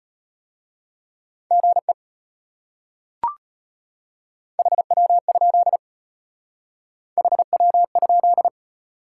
Morse code, heard as a keyed tone, about 31 wpm, 700 Hz. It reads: GE HW? HW?